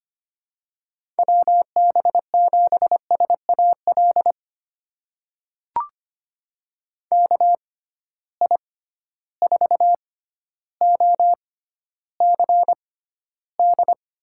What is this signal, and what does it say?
Morse code, heard as a keyed tone, about 25 wpm, spaced as for 13 wpm, 700 Hz. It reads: WB7SAL KI4OCD
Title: USA Call Signs - 6x Character Spacing - 25wpm